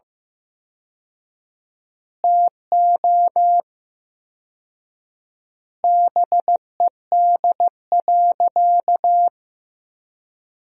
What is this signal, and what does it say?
Morse code, heard as a keyed tone, about 16 wpm, 700 Hz. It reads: TO BED.
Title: 4x Word-Spacing - Sentences from Top 500 Words - 15wpm